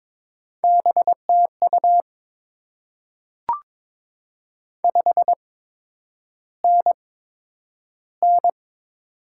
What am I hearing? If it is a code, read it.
Morse code, heard as a keyed tone, about 22 wpm, spaced as for 9 wpm, 700 Hz. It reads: BTU 5NN
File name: Top 20 QSO Elements - 8x Character Spacing - 22wpm